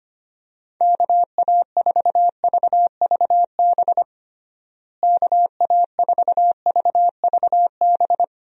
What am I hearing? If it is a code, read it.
Morse code, heard as a keyed tone, about 25 wpm, 700 Hz. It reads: KA4VVB KA4VVB
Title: USA Call Signs - 6 Characters - Repeated 2x - 25wpm